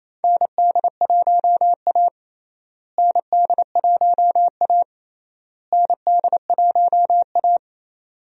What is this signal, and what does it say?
Morse code, heard as a keyed tone, about 28 wpm, 700 Hz. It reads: ND1A ND1A ND1A
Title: USA Call Signs - 4 Characters - Repeated 3x - 28wpm